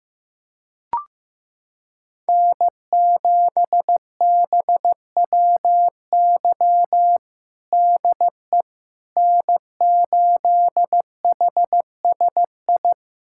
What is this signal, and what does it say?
Morse code, heard as a keyed tone, about 16 wpm, 700 Hz. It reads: N7BWY DE N8HSI
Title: USA Callsign DE Callsign - 5 Characters - 15wpm